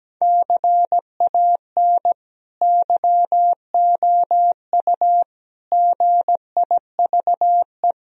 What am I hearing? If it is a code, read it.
Morse code, heard as a keyed tone, about 17 wpm, 700 Hz. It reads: CAN YOU GIVE
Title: Sentences from Top 1000 Words - Review Part 1 - 17wpm